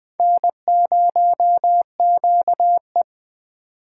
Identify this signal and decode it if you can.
Morse code, heard as a keyed tone, about 20 wpm, 700 Hz. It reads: N0QE
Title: Speed Racing - USA 1x2 Call Signs 17wpm